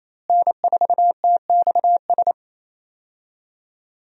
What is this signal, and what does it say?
Morse code, heard as a keyed tone, about 28 wpm, 700 Hz. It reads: N4TXS
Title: USA Call Signs - 5 Characters - Repeated 3x - 28wpm